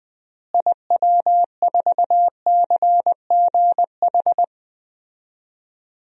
Morse code, heard as a keyed tone, about 20 wpm, 700 Hz. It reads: IW4CGH
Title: DXCC-100 Call Signs - 6 Characters - Repeated 3x - 20wpm